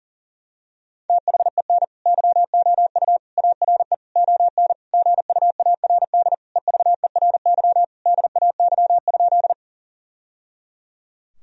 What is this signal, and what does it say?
Morse code, heard as a keyed tone, about 40 wpm, 700 Hz. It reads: THEN YOU ARE ON GUARD EVERY DAY?
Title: Sentences from Top 1000 Words - Review Part 4 - 40wpm